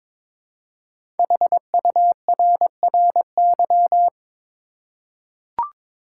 Morse code, heard as a keyed tone, about 22 wpm, 700 Hz. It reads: HURRY
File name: 5 Letter Words - First 3 Letters Given - 22wpm